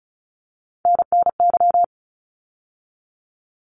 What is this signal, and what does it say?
Morse code, heard as a keyed tone, about 34 wpm, 700 Hz. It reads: NNY
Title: ARRL Sections 35wpm